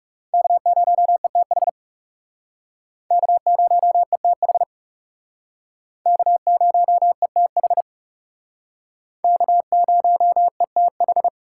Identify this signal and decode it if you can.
Morse code, heard as a keyed tone, about 38 wpm, 700 Hz. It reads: K0ETH K0ETH K0ETH K0ETH